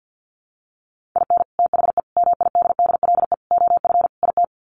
Morse code, heard as a keyed tone, about 50 wpm, 700 Hz. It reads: IN THE MIDDLE OF IT
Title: Sets of 5 Words Encore 50wpm